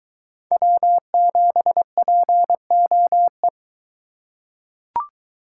Morse code, heard as a keyed tone, about 23 wpm, 700 Hz. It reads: W7POE